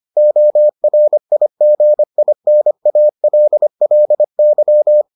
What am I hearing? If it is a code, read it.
Morse code, heard as a keyed tone, about 25 wpm, 600 Hz. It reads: ORIGINALLY